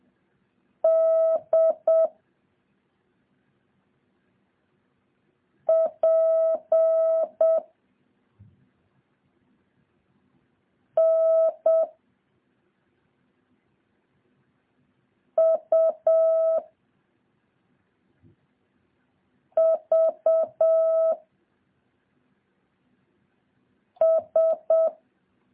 Morse code, heard as a keyed tone, about 7 wpm, 650 Hz. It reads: D P N U V S
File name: jh8cbh / 2023cw